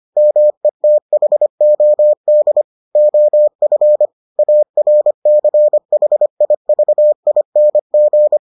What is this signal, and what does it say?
Morse code, heard as a keyed tone, about 25 wpm, 600 Hz. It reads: METHOD OF ARCHIVING